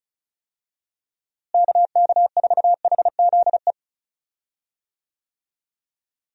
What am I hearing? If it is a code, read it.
Morse code, heard as a keyed tone, about 36 wpm, 700 Hz. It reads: KK4HZE
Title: USA Call Signs - 6 Characters - Repeated 2x - 35wpm